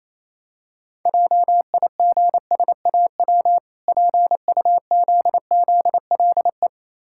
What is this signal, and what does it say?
Morse code, heard as a keyed tone, about 28 wpm, 700 Hz. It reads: JIGSAW PUZZLE